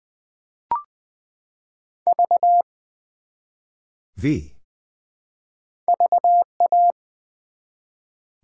Morse code, heard as a keyed tone, about 20 wpm, 700 Hz. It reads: V VA
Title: Build A Word - 1 Letter At A Time - 3 to 5-Letter Words - 20wpm